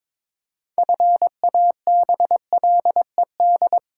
Morse code, heard as a keyed tone, about 22 wpm, 700 Hz. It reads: FABLED
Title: Pangram Challenge v2 - 22wpm